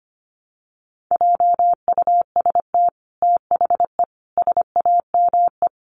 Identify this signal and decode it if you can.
Morse code, heard as a keyed tone, about 25 wpm, 700 Hz. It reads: JUST THE SAME
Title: Sentences from Top 900 Words 25wpm